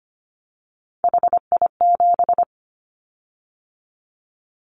Morse code, heard as a keyed tone, about 25 wpm, 700 Hz. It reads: HI7
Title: Callsign Prefixes for Top 100 DXCC - 25wpm